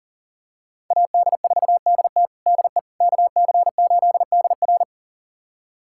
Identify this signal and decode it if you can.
Morse code, heard as a keyed tone, about 40 wpm, 700 Hz. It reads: AD4DT DE KC8DR
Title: USA Callsign DE Callsign - 5 Characters - 40wpm